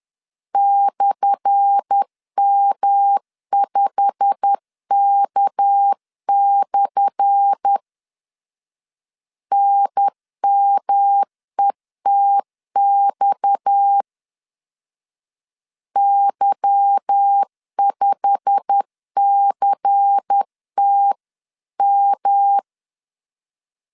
Morse code, heard as a keyed tone, about 10 wpm, 800 Hz. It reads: /M5K/ NMETX Y5CTM
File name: LEZIONI / 10 SIGNS / 10A SLASH 10wpm 2808